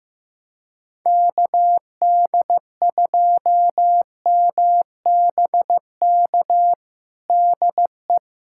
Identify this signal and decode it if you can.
Morse code, heard as a keyed tone, about 15 wpm, 700 Hz. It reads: KD2MBK DE
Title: USA Callsign DE Callsign 15wpm